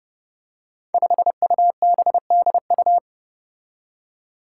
Morse code, heard as a keyed tone, about 30 wpm, 700 Hz. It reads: 5UBDU
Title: Five Challenging Letter-Number Combinations 30wpm